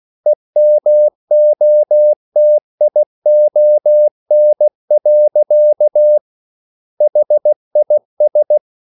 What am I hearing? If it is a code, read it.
Morse code, heard as a keyed tone, about 16 wpm, 600 Hz. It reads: EMOTION. HIS